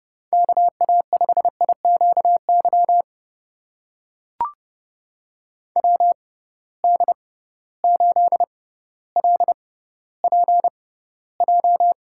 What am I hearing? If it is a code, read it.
Morse code, heard as a keyed tone, about 30 wpm, spaced as for 15 wpm, 700 Hz. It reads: KA5IQY WD8LPJ
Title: USA Call Signs - 6 Characters - 6x Character Spacing - 30wpm